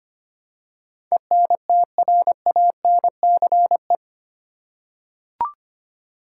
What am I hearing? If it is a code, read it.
Morse code, heard as a keyed tone, about 25 wpm, 700 Hz. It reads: ENTRANCE
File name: Eight Letter Words - 25wpm